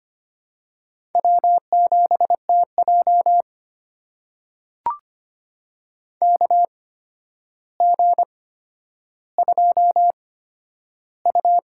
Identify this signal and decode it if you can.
Morse code, heard as a keyed tone, about 25 wpm, spaced as for 11 wpm, 700 Hz. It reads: W7TJ KG2U
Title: USA Call Signs - 4 Characters - 8x Character Spacing - 25wpm